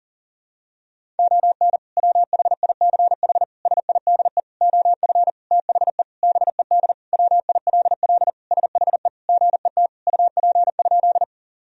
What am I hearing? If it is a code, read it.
Morse code, heard as a keyed tone, about 40 wpm, 700 Hz. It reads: ON WHICH SIDE OF THE BED WILL SHE GET UP?
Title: Sentences from Top 1000 Words - Review Part 1 - 40wpm